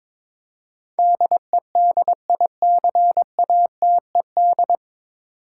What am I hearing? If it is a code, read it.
Morse code, heard as a keyed tone, about 22 wpm, 700 Hz. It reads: DEDICATED
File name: 9 Letter Words - First 3 Letters Given - 22wpm